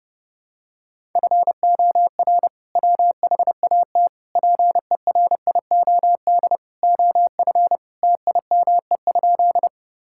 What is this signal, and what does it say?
Morse code, heard as a keyed tone, about 30 wpm, 700 Hz. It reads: FOR WHAT PERIOD OF TIME?
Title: Sentences from Top 1000 Words - Review Part 5 - 30wpm